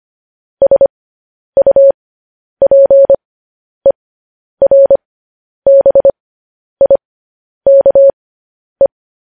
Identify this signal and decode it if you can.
Morse code, heard as a keyed tone, about 25 wpm, spaced as for 10 wpm, 550 Hz. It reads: SUPERBIKE
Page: News Headlines in Morse Code at 10 WPM